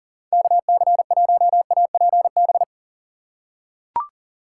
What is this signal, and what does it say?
Morse code, heard as a keyed tone, about 40 wpm, 700 Hz. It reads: KC1APB
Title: USA Call Signs - 6 Characters - 3x Character Spacing - 40wpm